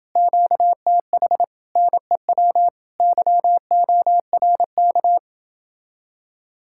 Morse code, heard as a keyed tone, about 29 wpm, 700 Hz. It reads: QTH NEW YORK